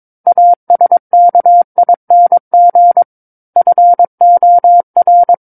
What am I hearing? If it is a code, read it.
Morse code, heard as a keyed tone, about 22 wpm, 700 Hz. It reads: ASKING FOR
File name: Book One / 13-Chapter 13 - 22 WPM